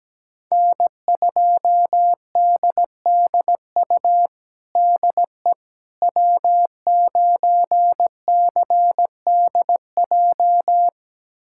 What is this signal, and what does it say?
Morse code, heard as a keyed tone, about 17 wpm, 700 Hz. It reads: N2DDU DE W9CDJ
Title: USA Callsign DE Callsign - 5 Characters - 17wpm